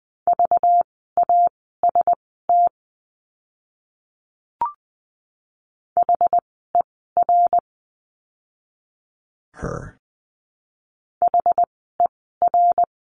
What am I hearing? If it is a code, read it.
Morse code, heard as a keyed tone, about 20 wpm, spaced as for 15 wpm, 700 Hz. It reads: VAST HER HER
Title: 192 Lesson 31f - Letter v - Words 20wpm